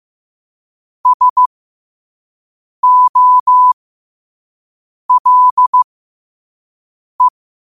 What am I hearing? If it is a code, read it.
Morse code, heard as a keyed tone, about 15 wpm, spaced as for 5 wpm, 1000 Hz. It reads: SOLE
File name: qotdfiles / 5wpm